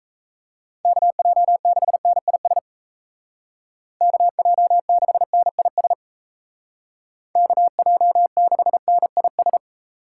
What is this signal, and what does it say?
Morse code, heard as a keyed tone, about 34 wpm, 700 Hz. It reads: KJ6NIS KJ6NIS KJ6NIS
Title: Speed Racing - USA 2x3 Call Signs 28wpm